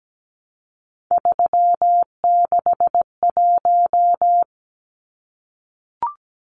Morse code, heard as a keyed tone, about 17 wpm, 700 Hz. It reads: 361
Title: Numbers - 3-Digit 17wpm